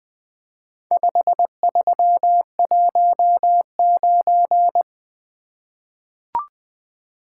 Morse code, heard as a keyed tone, about 20 wpm, 700 Hz. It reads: 5319
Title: Four Letter-Number Combinations - 20wpm